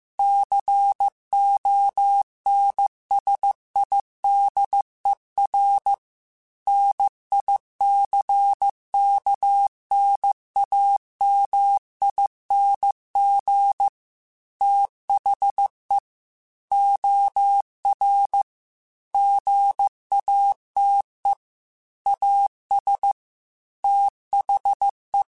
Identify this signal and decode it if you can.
Morse code, heard as a keyed tone, about 15 wpm, 800 Hz. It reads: CONSIDER NICKNAMING THE OR GATE AS THE
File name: Technical Subjects / Technical Subjects-mp3s / OR Gate @13